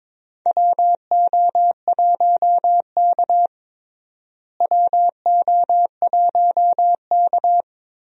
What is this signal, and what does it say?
Morse code, heard as a keyed tone, about 22 wpm, 700 Hz. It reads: WO1K WO1K
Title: USA Call Signs - 4 Characters - Repeated 2x - 22wpm